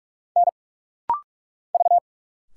Morse code, heard as a keyed tone, about 46 wpm, 690 Hz. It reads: N V
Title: Single Letters - Rapid-Fire 45wpm